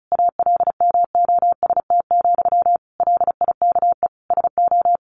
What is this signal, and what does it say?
Morse code, heard as a keyed tone, about 34 wpm, 700 Hz. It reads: ALMOST, LIKE SO
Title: A Sound Of Thunder 35wpm